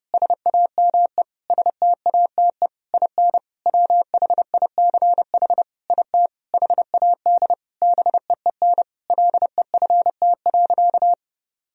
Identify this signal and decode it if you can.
Morse code, heard as a keyed tone, about 30 wpm, 700 Hz. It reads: SAME STATE IN WHICH IT HAD BEEN LEFT.
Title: Sentences from Top 1000 Words - Review Part 2 - 30wpm